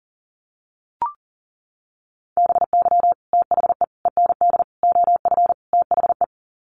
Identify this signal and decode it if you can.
Morse code, heard as a keyed tone, about 40 wpm, 700 Hz. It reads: BY THE END OF THE
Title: Sets of 5 Words 40wpm